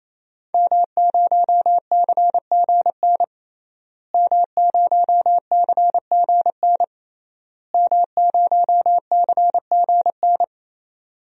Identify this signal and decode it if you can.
Morse code, heard as a keyed tone, about 27 wpm, 700 Hz. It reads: M0CGN M0CGN M0CGN
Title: DXCC-100 Call Signs - 5 Characters - Repeated 3x - 28wpm